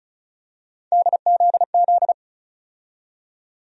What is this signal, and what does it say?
Morse code, heard as a keyed tone, about 34 wpm, 700 Hz. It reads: DZZ
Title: USA Call Sign Postfixes - 3 Letters 35wpm